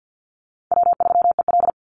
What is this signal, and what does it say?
Morse code, heard as a keyed tone, about 50 wpm, 700 Hz. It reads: W3EL
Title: Call Signs - 50wpm